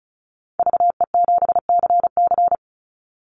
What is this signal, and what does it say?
Morse code, heard as a keyed tone, about 36 wpm, 700 Hz. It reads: VE7CC
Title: Call Signs - 35wpm